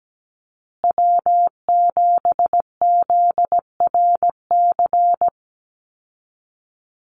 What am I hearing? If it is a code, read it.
Morse code, heard as a keyed tone, about 17 wpm, 700 Hz. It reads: W7ZRC
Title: Call Signs Repeated 3x - 17wpm